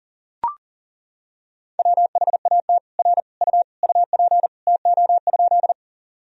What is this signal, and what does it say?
Morse code, heard as a keyed tone, about 40 wpm, 700 Hz. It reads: WHAT R U UP TO?
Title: Sentences from Top 100 Words and Abbreviations 40wpm